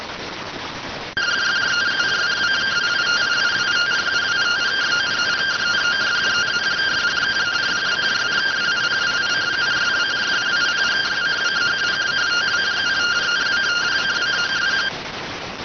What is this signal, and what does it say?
Two-tone teleprinter signal, about 170 Hz shift.